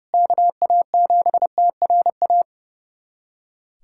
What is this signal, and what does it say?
Morse code, heard as a keyed tone, about 32 wpm, 700 Hz. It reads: KA7TRA
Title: USA Callsign DE Callsign 30wpm